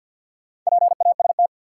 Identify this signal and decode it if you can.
Morse code, heard as a keyed tone, about 53 wpm, 700 Hz. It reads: PAST
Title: Sentences from Top 600 Words - New Word Drill - 50wpm